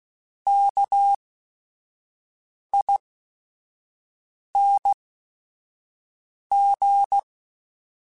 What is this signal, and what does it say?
Morse code, heard as a keyed tone, about 16 wpm, spaced as for 5 wpm, 800 Hz. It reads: KING
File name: Special INGs / Special INGs-mp3s / Sngl Ing @5